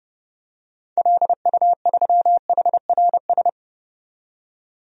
Morse code, heard as a keyed tone, about 30 wpm, 700 Hz. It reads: LU3HRS
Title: DXCC-100 Call Signs - 6 Characters - 30wpm